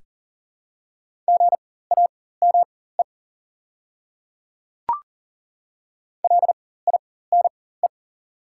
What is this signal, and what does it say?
Morse code, heard as a keyed tone, about 40 wpm, spaced as for 19 wpm, 700 Hz. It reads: GAME LINE
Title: 4x Character Spacing - Top 500 Words - 40wpm